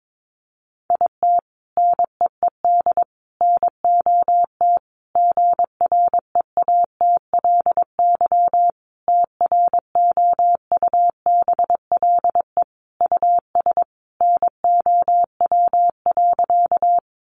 Morse code, heard as a keyed tone, about 22 wpm, 700 Hz. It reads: IT NEED NOT GREATLY TROUBLE US NOW.
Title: Sentences from Top 1400 Words 22wpm